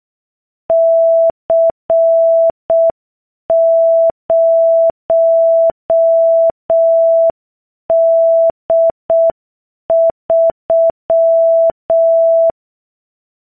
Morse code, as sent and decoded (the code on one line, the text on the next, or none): -.-. ----- -.. ...--
C0D3